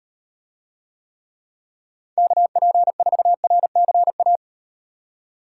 Morse code, heard as a keyed tone, about 38 wpm, 700 Hz. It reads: KP4RCA